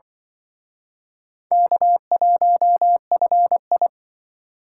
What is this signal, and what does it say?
Morse code, heard as a keyed tone, about 25 wpm, 700 Hz. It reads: K1FI